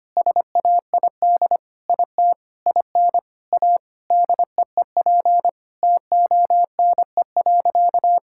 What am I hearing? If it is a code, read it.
Morse code, heard as a keyed tone, about 25 wpm, 700 Hz. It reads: SAID IT IN A DEEP TONE.
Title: Sentences from Top 1600 Words 25wpm